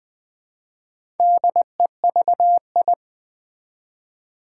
Morse code, heard as a keyed tone, about 20 wpm, 700 Hz. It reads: DEVI
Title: Build A Word - 1 Letter At A Time - 3 to 5-Letter Words - 20wpm